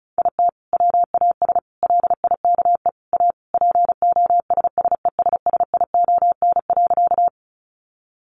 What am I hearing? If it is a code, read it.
Morse code, heard as a keyed tone, about 34 wpm, 700 Hz. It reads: IT WAS LIKE A POSSESSION.
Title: Sentences from Top 1400 Words 35wpm